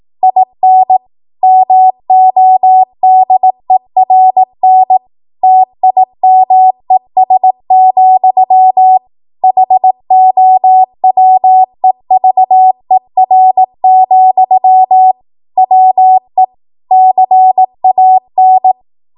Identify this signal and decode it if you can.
Morse code, heard as a keyed tone, about 18 wpm, 750 Hz. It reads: IN MODERN TIMES, HOWEVER, WE CAN
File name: Morse / 18WPM